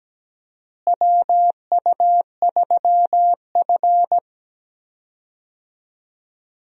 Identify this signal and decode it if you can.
Morse code, heard as a keyed tone, about 17 wpm, 700 Hz. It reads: WU3F
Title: USA Call Signs - 4 Characters - Repeated 3x - 17wpm